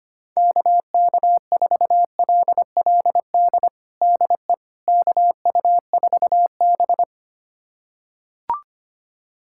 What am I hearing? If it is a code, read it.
Morse code, heard as a keyed tone, about 25 wpm, 700 Hz. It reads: KK4LLD DE KU4B
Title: USA Callsign DE Callsign 25wpm